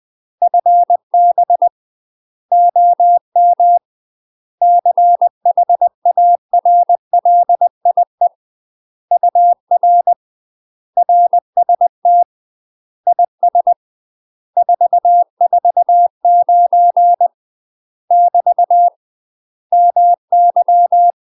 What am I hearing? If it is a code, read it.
Morse code, heard as a keyed tone, about 20 wpm, 700 Hz. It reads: FB OM CHARLIE UR RST IS 449 <BT> MY
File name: UK MASTER QSOs TRAINING / 20WPM / UK MASTER QSOs 0025